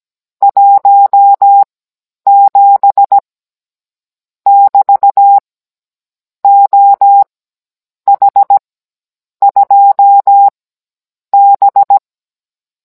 Morse code, heard as a keyed tone, about 17 wpm, spaced as for 8 wpm, 800 Hz. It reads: 17 <BT> OH2B
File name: Cours-NR18-16mm